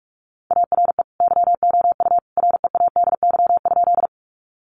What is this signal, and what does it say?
Morse code, heard as a keyed tone, about 44 wpm, 700 Hz. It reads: ARE YOU READY?
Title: Sentences from Top 400 Words 45wpm